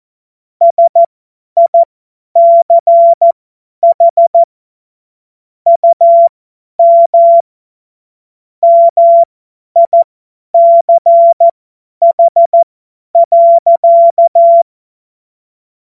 Morse code, heard as a keyed tone, about 14 wpm, spaced as for 10 wpm, 680 Hz. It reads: SICH UM MICH.
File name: Morsekurs / Witze / Witz 10 Juni 2025 50 F70